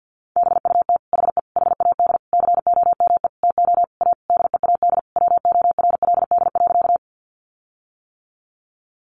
Morse code, heard as a keyed tone, about 50 wpm, 700 Hz. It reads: BUT HE HAD COME TO A DEAD WORLD.
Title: Sentences from Top 1500 Words 50wpm